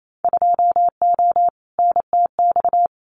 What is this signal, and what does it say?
Morse code, heard as a keyed tone, about 28 wpm, 700 Hz. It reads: 2O NTX
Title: Winter Field Day Exchanges 28wpm